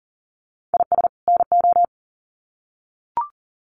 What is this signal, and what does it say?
Morse code, heard as a keyed tone, about 40 wpm, 700 Hz. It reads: IS NO
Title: Sets of 2 Words 40wpm